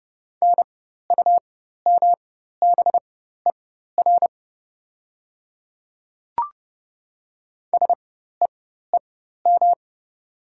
Morse code, heard as a keyed tone, about 30 wpm, spaced as for 14 wpm, 690 Hz. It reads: NUMBER SEEM
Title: 4x Character Spacing - Top 500 Words - 30wpm